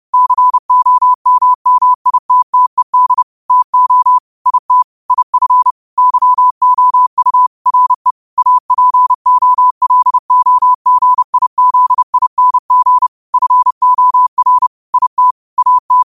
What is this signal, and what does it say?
Morse code, heard as a keyed tone, about 30 wpm, 1000 Hz. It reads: COMMITTED TO IT IF YOU RE APOLOGIZING FOR IT AT
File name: qotdfiles / 30wpm